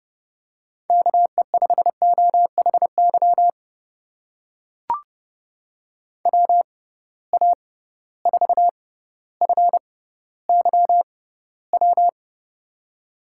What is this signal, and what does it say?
Morse code, heard as a keyed tone, about 30 wpm, spaced as for 15 wpm, 700 Hz. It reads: KE5OHY WA4FYW
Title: USA Call Signs - 6 Characters - 6x Character Spacing - 30wpm